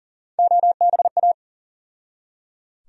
Morse code, heard as a keyed tone, about 39 wpm, 700 Hz. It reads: OBA
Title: USA Call Sign Postfixes - 3 Letters 40wpm